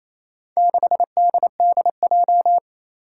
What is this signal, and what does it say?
Morse code, heard as a keyed tone, about 28 wpm, 700 Hz. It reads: 6DDJ
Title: Four Challenging Letter-Number Combinations 28wpm